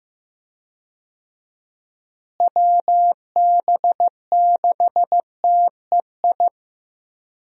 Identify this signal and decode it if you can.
Morse code, heard as a keyed tone, about 15 wpm, 700 Hz. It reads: WB6TEI
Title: USA Call Signs - 6 Characters - Repeated 2x - 15wpm